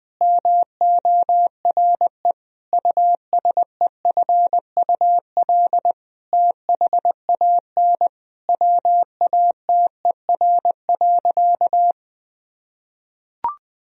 Morse code, heard as a keyed tone, about 21 wpm, 700 Hz. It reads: MORE USEFUL THAN WATER.
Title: Sentences from Top 1000 Words - Review Part 4 - 20wpm